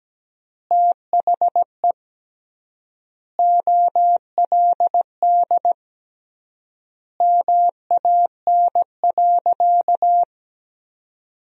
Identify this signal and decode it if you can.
Morse code, heard as a keyed tone, about 17 wpm, 700 Hz. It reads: THE OLD MAN.
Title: 3x Word-Spacing - Sentences from Top 200 Words - 17wpm